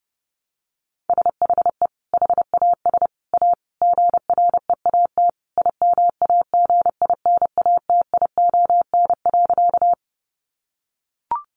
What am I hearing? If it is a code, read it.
Morse code, heard as a keyed tone, about 30 wpm, 700 Hz. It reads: SHE HAS A GREAT IMAGINATION.
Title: Sentences from Top 1300 Words 30wpm